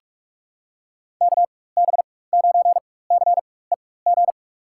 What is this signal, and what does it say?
Morse code, heard as a keyed tone, about 42 wpm, spaced as for 20 wpm, 700 Hz. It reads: KB9CEG
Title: USA Call Signs - 6 Characters - 4x Character Spacing - 45wpm